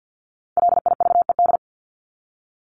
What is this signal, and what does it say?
Morse code, heard as a keyed tone, about 50 wpm, 700 Hz. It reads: LIVED
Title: ED Suffix - Top 100 Words with Base Words 50wpm